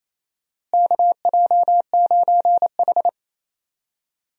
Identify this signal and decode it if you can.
Morse code, heard as a keyed tone, about 28 wpm, 700 Hz. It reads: KJ9H
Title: USA Call Signs - 4 Characters - Repeated 2x - 28wpm